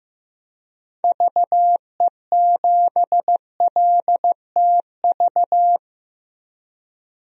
Morse code, heard as a keyed tone, about 15 wpm, 700 Hz. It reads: VE7LTV
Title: DXCC-100 Call Signs - 6 Characters - 15wpm